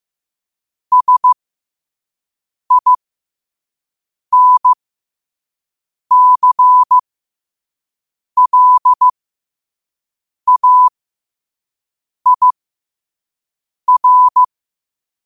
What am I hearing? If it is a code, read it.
Morse code, heard as a keyed tone, about 15 wpm, spaced as for 5 wpm, 1000 Hz. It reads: SINCLAIR
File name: qotdfiles / 5wpm